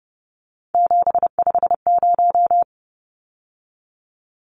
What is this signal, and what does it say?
Morse code, heard as a keyed tone, about 30 wpm, 700 Hz. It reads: 750
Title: Numbers - 3-Digit 30wpm